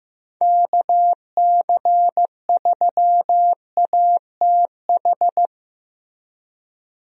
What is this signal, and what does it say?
Morse code, heard as a keyed tone, about 15 wpm, 700 Hz. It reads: KC3ATH
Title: USA Call Signs - 6 Characters - Repeated 2x - 15wpm